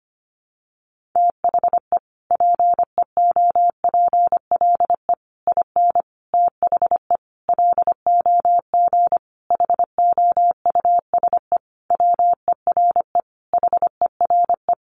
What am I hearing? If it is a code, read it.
Morse code, heard as a keyed tone, about 25 wpm, 700 Hz. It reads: THE PEOPLE IN THE LOG HOUSE WERE HERE
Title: Sentences from Top 1400 Words 25wpm